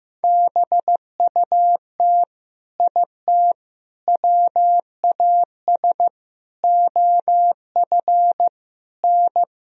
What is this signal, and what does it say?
Morse code, heard as a keyed tone, about 15 wpm, 700 Hz. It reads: BUT IT WAS OF N
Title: Sentences from Top 100 Words and Abbreviations 15wpm